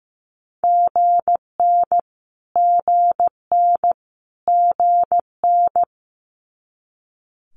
Morse code, heard as a keyed tone, about 15 wpm, 700 Hz. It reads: GN GN GN
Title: Multi-letter Phonemes Repeated 3x 15wpm